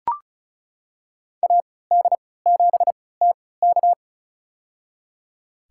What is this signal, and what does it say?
Morse code, heard as a keyed tone, about 34 wpm, spaced as for 20 wpm, 700 Hz. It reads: AD7TK
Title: USA Call Signs - 5 Characters - 3x Character Spacing - 35wpm